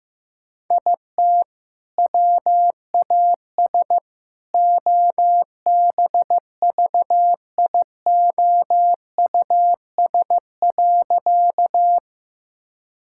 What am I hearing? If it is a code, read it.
Morse code, heard as a keyed tone, about 15 wpm, 700 Hz. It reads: IT WAS OBVIOUS.
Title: Sentences from Top 1600 Words 15wpm